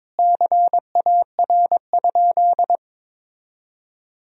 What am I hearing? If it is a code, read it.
Morse code, heard as a keyed tone, about 22 wpm, 700 Hz. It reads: CAR?
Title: Sentences from Top 1000 Words - Review Part 1 - 22wpm